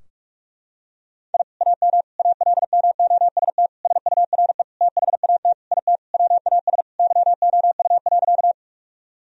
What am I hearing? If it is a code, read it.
Morse code, heard as a keyed tone, about 48 wpm, 700 Hz. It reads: I AM ALMOST SURE THAT IT WAS YOU.
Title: Sentences from Top 1000 Words - Review Part 4 - 45wpm